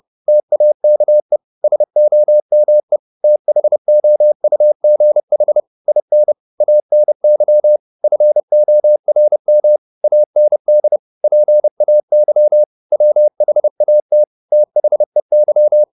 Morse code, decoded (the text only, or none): TAKE SOME THOUGH IN ANY FORM AND PAY WHAT THEY